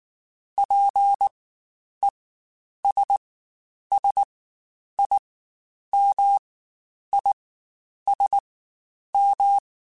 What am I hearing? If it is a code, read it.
Morse code, heard as a keyed tone, about 19 wpm, spaced as for 9 wpm, 800 Hz. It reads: PESSIMISM
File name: Quotes Spaced / Quotes Spaced-mp3s / Pessimism sp @18